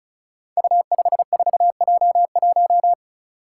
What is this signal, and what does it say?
Morse code, heard as a keyed tone, about 36 wpm, 700 Hz. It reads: U54J1
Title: Five Challenging Letter-Number Combinations 35wpm